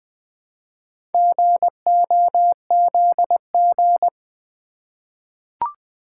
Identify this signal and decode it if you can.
Morse code, heard as a keyed tone, about 20 wpm, 700 Hz. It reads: GOZG
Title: Four Letter-Number Combinations - 20wpm